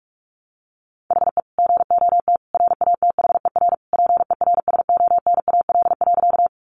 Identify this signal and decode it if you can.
Morse code, heard as a keyed tone, about 44 wpm, 700 Hz. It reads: HE GOT RATHER PERSONAL.
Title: Sentences from Top 700 Words 45wpm